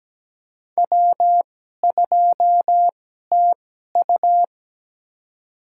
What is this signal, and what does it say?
Morse code, heard as a keyed tone, about 17 wpm, spaced as for 12 wpm, 700 Hz. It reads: W2TU
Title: USA Call Signs - 4 Characters - 2x Character Spacing - 17wpm